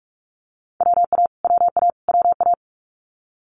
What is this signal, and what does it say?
Morse code, heard as a keyed tone, about 46 wpm, 700 Hz. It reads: WA WA WA
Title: Multi-letter Phonemes Repeated 3x 45wpm